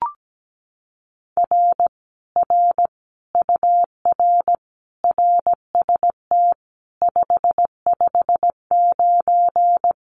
Morse code, heard as a keyed tone, about 17 wpm, 700 Hz. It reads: R R UR RST 559
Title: Common QSO Phrases Encore 17wpm